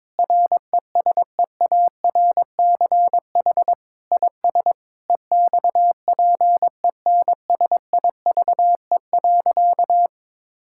Morse code, heard as a keyed tone, about 22 wpm, 700 Hz. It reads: RESEARCH IS EXPENSIVE.
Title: Sentences from Top 1000 Words - Review Part 6 - 22wpm